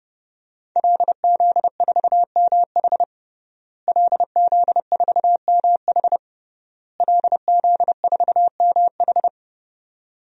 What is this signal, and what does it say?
Morse code, heard as a keyed tone, about 32 wpm, 700 Hz. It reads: LZ4MH LZ4MH LZ4MH
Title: DXCC-100 Call Signs - 5 Characters - Repeated 3x - 30wpm